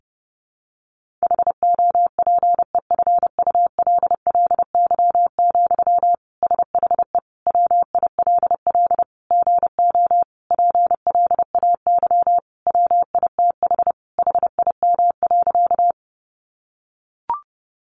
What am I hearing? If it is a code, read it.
Morse code, heard as a keyed tone, about 30 wpm, 700 Hz. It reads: HOPEFULLY, SHE WILL GO PLAY WITH HIM.
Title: Sentences from Top 1500 Words 30wpm